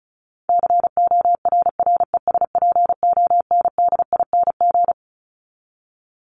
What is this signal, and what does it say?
Morse code, heard as a keyed tone, about 34 wpm, 700 Hz. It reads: CORRESPONDING
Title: ING Suffix - Top 100 Words with Base Words 35wpm